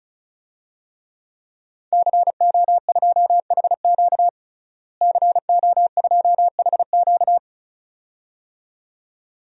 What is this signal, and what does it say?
Morse code, heard as a keyed tone, about 34 wpm, 700 Hz. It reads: CO2HQ CO2HQ
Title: DXCC-100 Call Signs - 5 Characters - Repeated 2x - 35wpm